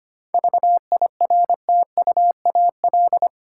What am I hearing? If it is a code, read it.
Morse code, heard as a keyed tone, about 25 wpm, 700 Hz. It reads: VIRTUAL
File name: practice / 25wpm / Difficult Words with Hints - 25wpm